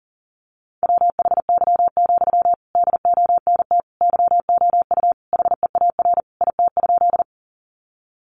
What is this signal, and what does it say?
Morse code, heard as a keyed tone, about 40 wpm, 700 Hz. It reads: WHY, DONT YOU HEAR IT?
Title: Sentences from Top 300 Words 40wpm